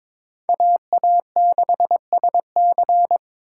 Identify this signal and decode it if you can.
Morse code, heard as a keyed tone, about 26 wpm, 700 Hz. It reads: AA6SC
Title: Speed Racing - USA 2x2 Call Signs 22wpm